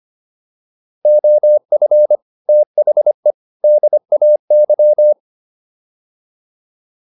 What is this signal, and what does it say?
Morse code, heard as a keyed tone, about 25 wpm, 600 Hz. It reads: OF THE DAY
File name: WD205 25